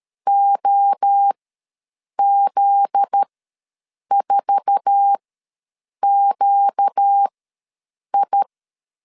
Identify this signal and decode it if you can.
Morse code, heard as a keyed tone, about 13 wpm, spaced as for 7 wpm, 800 Hz. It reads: OZ4QI
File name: LEZIONI / 8 YJ45 / 8 YJ4 ws 12wpm 3249